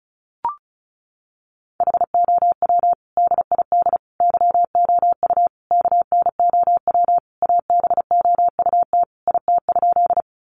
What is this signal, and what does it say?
Morse code, heard as a keyed tone, about 34 wpm, 700 Hz. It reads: HOW DID YOU KNOW ABOUT IT?
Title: Sentences from Top 100 Words 35wpm